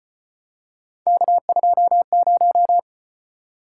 Morse code, heard as a keyed tone, about 32 wpm, 700 Hz. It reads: K20